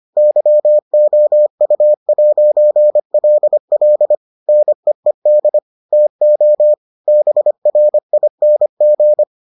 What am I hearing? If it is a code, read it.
Morse code, heard as a keyed tone, about 25 wpm, 600 Hz. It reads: YOU'LL NEED TO BRING